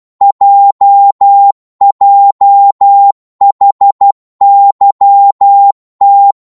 Morse code, heard as a keyed tone, about 12 wpm, 800 Hz. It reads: JJHYT
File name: Morse / ltr-12wpm